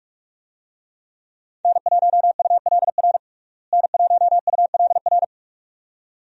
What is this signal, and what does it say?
Morse code, heard as a keyed tone, about 42 wpm, 700 Hz. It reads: N1ULR N1ULR
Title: USA Call Signs - 5 Characters - Repeated 2x - 45wpm